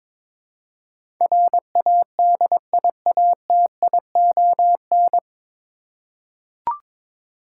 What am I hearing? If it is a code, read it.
Morse code, heard as a keyed tone, about 22 wpm, 700 Hz. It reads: RADIATION
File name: 9 Letter Words - First 3 Letters Given - 22wpm